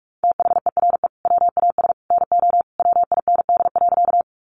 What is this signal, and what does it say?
Morse code, heard as a keyed tone, about 44 wpm, 700 Hz. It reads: THERE WAS NO WIND.
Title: Sentences from Top 400 Words 45wpm